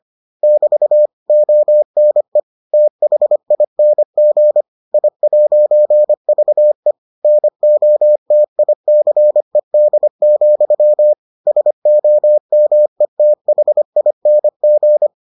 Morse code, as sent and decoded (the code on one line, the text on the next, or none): -...- / --- -. / - .... .. -. --. / .. .----. ...- . / -. --- - .. -.-. -.. --..-- / ... --- -- . - .... .. -. --.
<BT> ONE THING I'VE NOTICED, SOMETHING